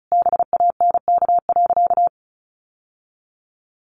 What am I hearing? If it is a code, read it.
Morse code, heard as a keyed tone, about 34 wpm, 700 Hz. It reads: BANK.
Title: Sentences from Top 1200 Words 35wpm